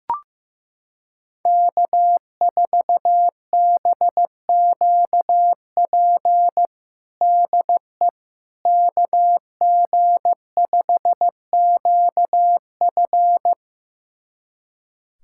Morse code, heard as a keyed tone, about 15 wpm, 700 Hz. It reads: K4BQP DE KG5QF
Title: USA Callsign DE Callsign 15wpm